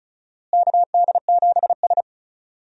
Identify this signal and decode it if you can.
Morse code, heard as a keyed tone, about 36 wpm, 700 Hz. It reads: KD7S
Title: USA Call Signs - 4 Characters - Repeated 3x - 35wpm